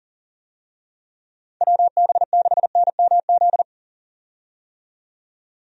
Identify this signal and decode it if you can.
Morse code, heard as a keyed tone, about 40 wpm, 700 Hz. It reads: WB6NMZ
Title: USA Call Signs - 6 Characters - 40wpm